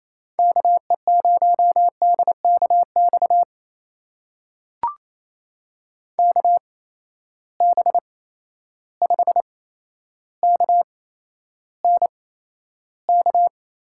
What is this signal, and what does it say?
Morse code, heard as a keyed tone, about 28 wpm, spaced as for 12 wpm, 700 Hz. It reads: KE0DKX KB5KNK
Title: USA Call Signs - 6 Characters - 8x Character Spacing - 28wpm